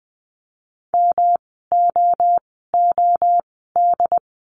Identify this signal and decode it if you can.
Morse code, heard as a keyed tone, about 20 wpm, spaced as for 15 wpm, 700 Hz. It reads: MOOD
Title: 108 Lesson 18f - Letter m - Words 20wpm